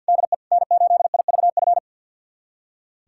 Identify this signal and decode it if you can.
Morse code, heard as a keyed tone, about 46 wpm, 700 Hz. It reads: DE N8IVF
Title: USA Callsign DE Callsign 50wpm